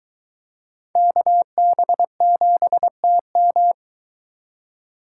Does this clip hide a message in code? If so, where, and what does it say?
Yes: Morse code from 1.0 to 3.7 s, heard as a keyed tone, about 22 wpm, 700 Hz. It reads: KB7TM